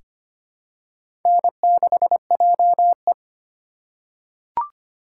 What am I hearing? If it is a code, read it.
Morse code, heard as a keyed tone, about 25 wpm, 700 Hz. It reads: N6JE
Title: USA Call Signs - 4 Characters - 4x Character Spacing - 25wpm